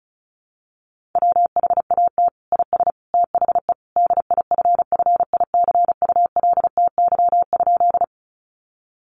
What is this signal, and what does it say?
Morse code, heard as a keyed tone, about 34 wpm, 700 Hz. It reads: WHAT IS THE DIFFICULTY?
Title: Sentences from Top 1500 Words 35wpm